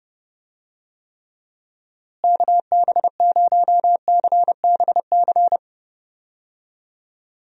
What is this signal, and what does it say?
Morse code, heard as a keyed tone, about 30 wpm, 700 Hz. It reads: KB0CBC
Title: USA Call Signs - 6 Characters - Repeated 2x - 30wpm